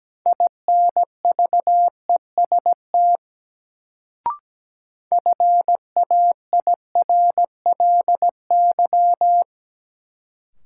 Morse code, heard as a keyed tone, about 17 wpm, 700 Hz. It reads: INVEST FAIRLY
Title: Sentences from Top 1100 Words - New Word Drill - 17wpm